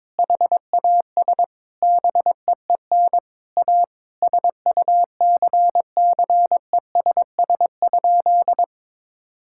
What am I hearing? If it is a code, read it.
Morse code, heard as a keyed tone, about 21 wpm, 700 Hz. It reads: HAS BEEN A SUCCESS?
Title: Sentences from Top 1000 Words - Review Part 6 - 22wpm